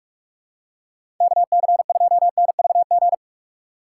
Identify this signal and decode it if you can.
Morse code, heard as a keyed tone, about 47 wpm, 700 Hz. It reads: KC2NVG
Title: USA Call Signs - 6 Characters - 2x Character Spacing - 45wpm